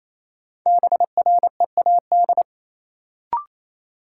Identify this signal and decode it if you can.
Morse code, heard as a keyed tone, about 28 wpm, 700 Hz. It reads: BREAD
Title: Sentences from Top 1600 Words - New Word Drill - 28wpm